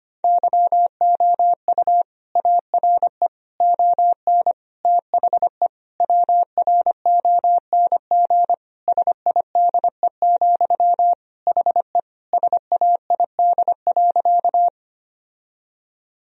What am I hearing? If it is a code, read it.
Morse code, heard as a keyed tone, about 25 wpm, 700 Hz. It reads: YOU ARE ON THE WRONG SIDE, HE SAID.
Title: Sentences from Top 1000 Words - Review Part 1 - 25wpm